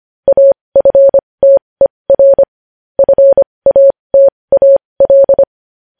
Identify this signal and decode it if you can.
Morse code, heard as a keyed tone, about 25 wpm, spaced as for 20 wpm, 550 Hz. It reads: AFTER FATAL
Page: News Headlines in Morse Code at 20 WPM